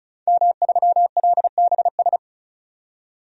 Morse code, heard as a keyed tone, about 34 wpm, 700 Hz. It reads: M3LBS
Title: DXCC-100 Call Signs - 5 Characters - 35wpm